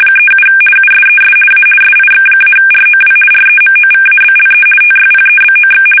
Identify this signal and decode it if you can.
RTTY, about 50 baud. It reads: 37 USES A SIX-ROW REMOVABLE TYPEBOX
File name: fsk850-50B / teletype-corp